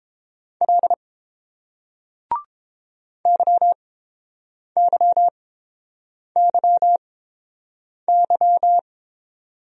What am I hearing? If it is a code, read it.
Morse code, heard as a keyed tone, about 30 wpm, spaced as for 7 wpm, 700 Hz. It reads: L YYYY